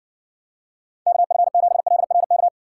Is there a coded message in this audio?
Yes: Morse code from 1.1 to 2.5 s, heard as a keyed tone, about 60 wpm, 700 Hz. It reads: KF7LRK